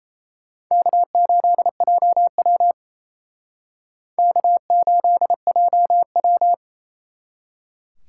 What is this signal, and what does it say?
Morse code, heard as a keyed tone, about 34 wpm, 700 Hz. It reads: K8JW K8JW
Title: Speed Racing - USA 1x2 Call Signs 28wpm